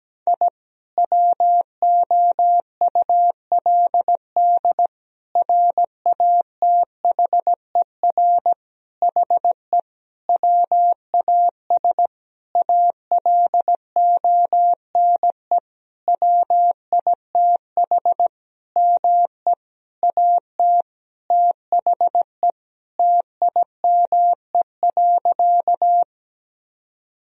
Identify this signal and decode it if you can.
Morse code, heard as a keyed tone, about 17 wpm, 700 Hz. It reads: I WOULD RATHER HE WAS ALONE WITH ME AT THE TIME.
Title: Sentences from Top 1000 Words - Review Part 4 - 17wpm